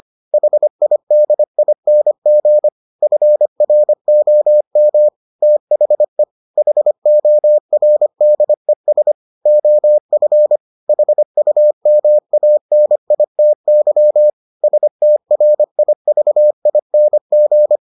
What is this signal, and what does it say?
Morse code, heard as a keyed tone, about 25 wpm, 600 Hz. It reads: HIDING FROM THE HORDES OF HUMANITY STRIVING